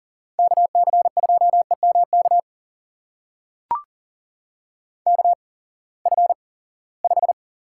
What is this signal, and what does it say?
Morse code, heard as a keyed tone, about 41 wpm, spaced as for 23 wpm, 700 Hz. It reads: KC2EMK KF5
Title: USA Call Signs - 6 Characters - 8x Character Spacing - 40wpm